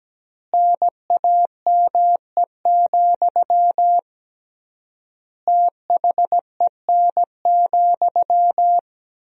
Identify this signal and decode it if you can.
Morse code, heard as a keyed tone, about 17 wpm, 700 Hz. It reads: NAME, THEN,
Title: 3x Word-Spacing - Sentences from Top 500 Words - 17wpm